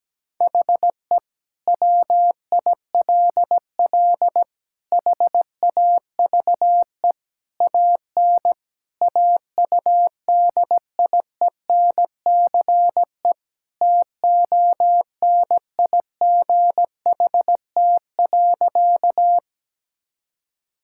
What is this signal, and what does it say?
Morse code, heard as a keyed tone, about 17 wpm, 700 Hz. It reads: HE WILL HAVE AN AUDIENCE TONIGHT.
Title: Sentences from Top 1000 Words - Review Part 5 - 17wpm